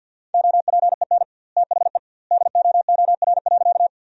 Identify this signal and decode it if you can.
Morse code, heard as a keyed tone, about 50 wpm, 700 Hz. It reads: OPEN THE DOOR.
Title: 2x Word-Spacing - Sentences from Top 300 Words - 50wpm